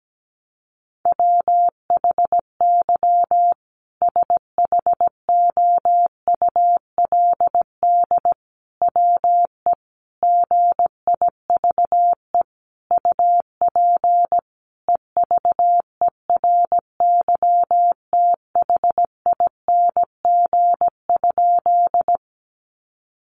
Morse code, heard as a keyed tone, about 17 wpm, 700 Hz. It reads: WHY SHOULD WE GIVE UP EVERYTHING?
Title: Sentences from Top 700 Words 17wpm